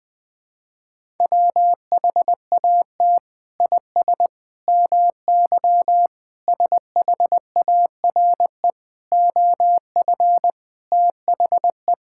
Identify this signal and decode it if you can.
Morse code, heard as a keyed tone, about 21 wpm, 700 Hz. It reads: WHAT IS MY SHARE OF THE
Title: Sentences from Top 1000 Words - Review Part 6 - 20wpm